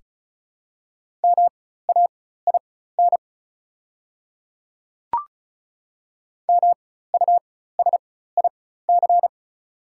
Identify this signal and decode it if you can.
Morse code, heard as a keyed tone, about 34 wpm, spaced as for 16 wpm, 690 Hz. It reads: MAIN MUSIC
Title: 4x Character Spacing - Top 300 Words - 35wpm